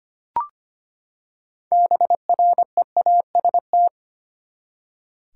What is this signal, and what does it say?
Morse code, heard as a keyed tone, about 25 wpm, 700 Hz. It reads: BREAST
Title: Six Letter Words - 25wpm